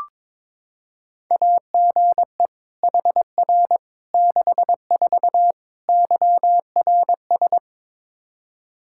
Morse code, heard as a keyed tone, about 22 wpm, 700 Hz. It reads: AGE HR 64 YRS
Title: K1USN SST QSO Practice 22wpm